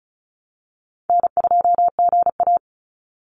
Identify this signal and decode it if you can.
Morse code, heard as a keyed tone, about 34 wpm, 700 Hz. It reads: N2GA
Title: Call Signs - 35wpm